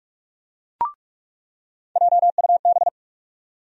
Morse code, heard as a keyed tone, about 44 wpm, 700 Hz. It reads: JUB
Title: USA Call Sign Postfixes - 3 Letters 45wpm